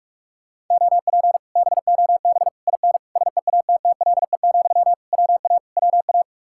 Morse code, heard as a keyed tone, about 50 wpm, 700 Hz. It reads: OP BOB IN SEATTLE, WA WA